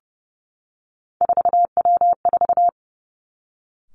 Morse code, heard as a keyed tone, about 32 wpm, 700 Hz. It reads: 4W4
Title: Three Challenging Letter-Number Combinations 30wpm